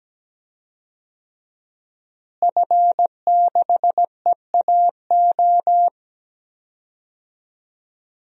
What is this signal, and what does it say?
Morse code, heard as a keyed tone, about 17 wpm, 700 Hz. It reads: F6EAO
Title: DXCC-100 Call Signs - 5 Characters - Repeated 2x - 17wpm